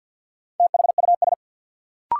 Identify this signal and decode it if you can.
Morse code, heard as a keyed tone, about 46 wpm, 700 Hz. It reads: THUS
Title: Sentences from Top 800 Words - New Word Drill - 50wpm